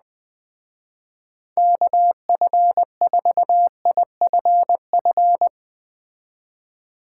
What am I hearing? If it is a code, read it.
Morse code, heard as a keyed tone, about 21 wpm, 700 Hz. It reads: KF4IFF